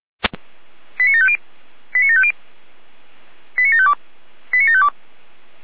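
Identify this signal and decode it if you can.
Five-tone selective call (ZVEI), heard as quick runs of five tones, tones of about 70 ms. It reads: 89750 89750 89752 89752
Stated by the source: ZVEI five-tone selective call 89518 received by SDR.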